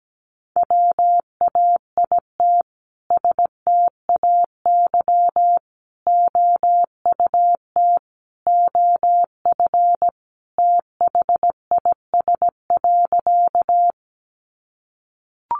Morse code, heard as a keyed tone, about 17 wpm, 700 Hz. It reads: WAIT STAY OUT OF THIS.
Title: A Sound Of Thunder 17wpm